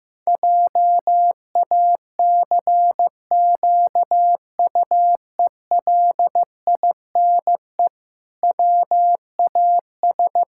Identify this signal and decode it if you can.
Morse code, heard as a keyed tone, about 15 wpm, 700 Hz. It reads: JACQUELINE WAS
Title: Pangram Challenge v2 - 15wpm